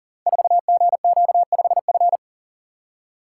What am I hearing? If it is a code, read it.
Morse code, heard as a keyed tone, about 39 wpm, 700 Hz. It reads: 4GQ5F
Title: Code Groups - 5 Letter:Numbers - 40wpm